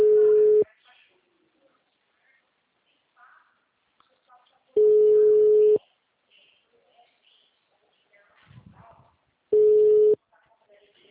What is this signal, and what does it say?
Ringback tone (Europe, 425 Hz), with speech.
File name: out- -2008- - - .42026